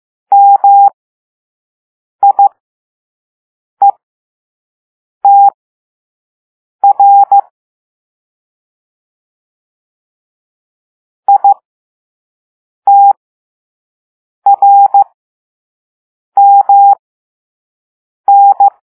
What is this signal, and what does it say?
Morse code, heard as a keyed tone, about 15 wpm, spaced as for 5 wpm, 800 Hz. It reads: MIETR ITRMN